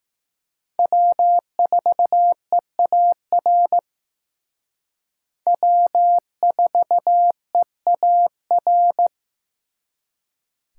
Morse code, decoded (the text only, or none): W4EAR W4EAR